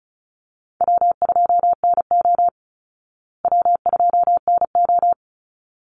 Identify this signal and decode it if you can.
Morse code, heard as a keyed tone, about 34 wpm, 700 Hz. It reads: W2NO W2NO
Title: Easy Call Signs Repeated 2x - 35wpm